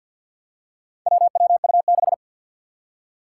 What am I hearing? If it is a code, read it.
Morse code, heard as a keyed tone, about 50 wpm, 700 Hz. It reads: WWU6
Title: Four Challenging Letter-Number Combinations 50wpm